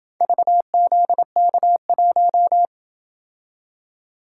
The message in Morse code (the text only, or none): VZK1